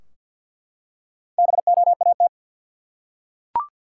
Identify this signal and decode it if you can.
Morse code, heard as a keyed tone, about 48 wpm, 700 Hz. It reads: BOAT
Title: Four Letter Words - 50wpm